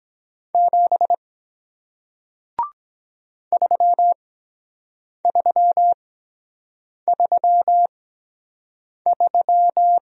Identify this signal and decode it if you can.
Morse code, heard as a keyed tone, about 23 wpm, spaced as for 7 wpm, 700 Hz. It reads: 7 3333